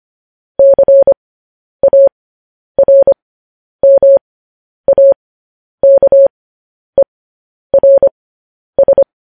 Morse code, heard as a keyed tone, about 25 wpm, spaced as for 10 wpm, 550 Hz. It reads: CARMAKERS
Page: news-headlines-in-morse-code-at-10-wpm